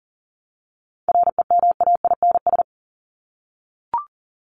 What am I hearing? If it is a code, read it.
Morse code, heard as a keyed tone, about 40 wpm, 700 Hz. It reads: REMAINS
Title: ES:S Suffix - Top 100 Words with Base Words 40wpm